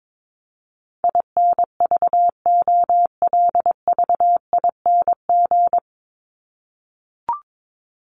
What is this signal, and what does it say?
Morse code, heard as a keyed tone, about 22 wpm, 700 Hz. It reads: INVOLVING
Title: ING Suffix - Top 100 Words with Base Words 22wpm